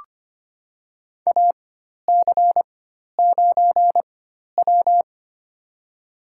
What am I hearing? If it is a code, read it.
Morse code, heard as a keyed tone, about 25 wpm, spaced as for 12 wpm, 700 Hz. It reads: AC9W
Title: USA Call Signs - 4 Characters - 4x Character Spacing - 25wpm